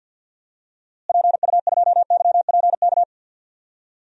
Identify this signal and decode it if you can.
Morse code, heard as a keyed tone, about 47 wpm, 700 Hz. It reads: PU2YPK
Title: DXCC-100 Call Signs - 6 Characters - Repeated 2x - 50wpm